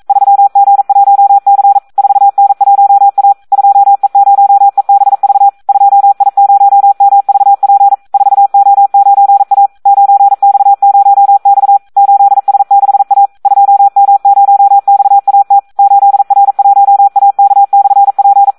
Morse code, heard as a keyed tone, about 42 wpm, 800 Hz. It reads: 3G1C 4N1A 2E0EBV 2I0MVP 4O9A 9X0X 8S6A 2M0XAT 8R1AK/P
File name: Audio practica CW / SD 1 42